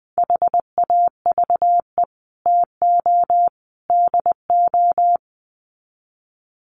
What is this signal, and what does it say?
Morse code, heard as a keyed tone, about 20 wpm, 700 Hz. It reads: HAVE TO DO
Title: Sets of 3 Words 20wpm